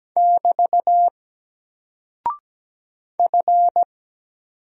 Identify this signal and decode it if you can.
Morse code, heard as a keyed tone, about 17 wpm, 700 Hz. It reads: <BT> F
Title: Single Character (Letter:Number:Punctuation:Prosign) 17wpm